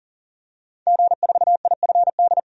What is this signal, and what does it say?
Morse code, heard as a keyed tone, about 40 wpm, 700 Hz. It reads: G4IFD
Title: DXCC-100 Call Signs - 5 Characters - 40wpm